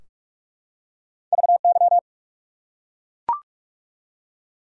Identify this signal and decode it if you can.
Morse code, heard as a keyed tone, about 43 wpm, 700 Hz. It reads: VY
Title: Top 90 QSO Elements 45wpm